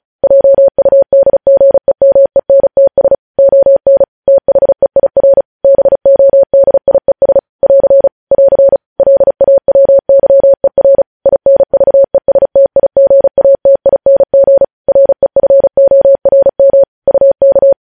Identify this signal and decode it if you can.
Morse code, heard as a keyed tone, about 36 wpm, 550 Hz. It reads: JUDGEMENTS ON THEIR BODIES <AR> <AR> LAWYER INVESTIGATING REFORM UK
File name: podcast / News headlines-1x-35x35